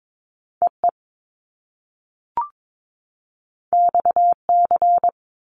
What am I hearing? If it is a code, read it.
Morse code, heard as a keyed tone, about 22 wpm, 700 Hz. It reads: EE XC
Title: Top Two Letter Combinations in Words 22wpm